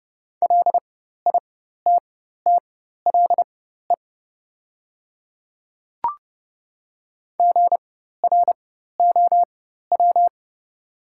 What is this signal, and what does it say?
Morse code, heard as a keyed tone, about 30 wpm, spaced as for 14 wpm, 690 Hz. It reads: LITTLE GROW
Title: 4x Character Spacing - Top 300 Words - 30wpm